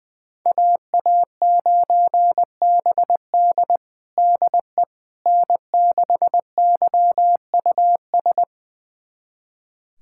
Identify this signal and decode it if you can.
Morse code, heard as a keyed tone, about 20 wpm, 700 Hz. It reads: AA9BD DE N6YUS
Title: USA Callsign DE Callsign 20wpm